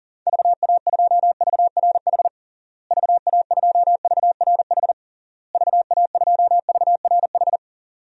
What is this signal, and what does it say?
Morse code, heard as a keyed tone, about 40 wpm, 700 Hz. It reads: VA2VRH VA2VRH VA2VRH
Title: DXCC-100 Call Signs - 6 Characters - Repeated 3x - 40wpm